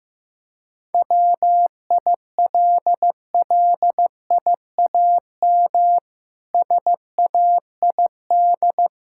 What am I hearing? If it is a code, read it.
Morse code, heard as a keyed tone, about 15 wpm, 700 Hz. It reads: WILLIAM SAID
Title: Pangram Challenge v2 - 15wpm